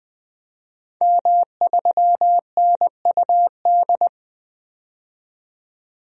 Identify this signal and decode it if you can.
Morse code, heard as a keyed tone, about 20 wpm, 700 Hz. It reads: M3NUD
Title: DXCC-100 Call Signs - 5 Characters - 20wpm